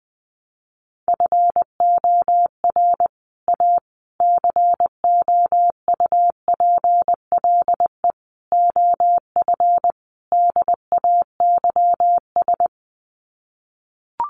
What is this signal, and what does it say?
Morse code, heard as a keyed tone, about 20 wpm, 700 Hz. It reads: FOR A COUPLE OF DAYS
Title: Sets of 5 Words Encore2 20wpm